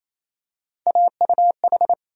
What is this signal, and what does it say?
Morse code, heard as a keyed tone, about 28 wpm, 700 Hz. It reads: AUH
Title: USA Call Sign Postfixes - 3 Letters 28wpm